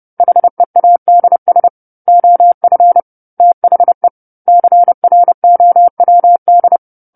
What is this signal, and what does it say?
Morse code, heard as a keyed tone, about 30 wpm, 700 Hz. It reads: HEADS OF THE CROWD
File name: 10-Chapter-10 30 WPM